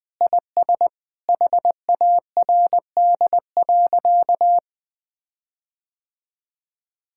Morse code, heard as a keyed tone, about 20 wpm, 700 Hz. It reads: IS HARD.
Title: Sentences with 6-Word Phrases - Spoken Beginning - 20wpm